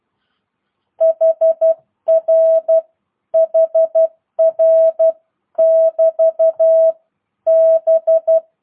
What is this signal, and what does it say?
Morse code, heard as a keyed tone, about 12 wpm, 650 Hz. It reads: HR HR <BT> B